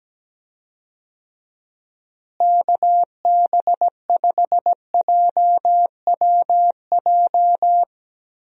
Morse code, heard as a keyed tone, about 17 wpm, 700 Hz. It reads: KB5JWJ
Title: USA Call Signs - 6 Characters - Repeated 2x - 17wpm